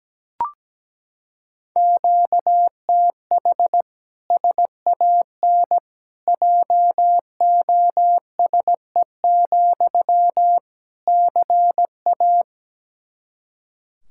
Morse code, heard as a keyed tone, about 17 wpm, 700 Hz. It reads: QTH SAN JOSE, CA
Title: Practice QSO - Exchange 1 - Beginning - 17wpm